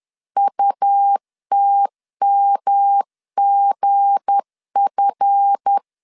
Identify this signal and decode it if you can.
Morse code, heard as a keyed tone, about 10 wpm, 800 Hz. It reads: UTMGF
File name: LEZIONI / 7 GF9Z / 7 GF 10wpm 2514